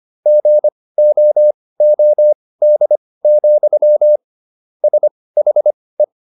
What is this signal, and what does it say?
Morse code, heard as a keyed tone, about 25 wpm, spaced as for 18 wpm, 600 Hz. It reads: GOOD, SHE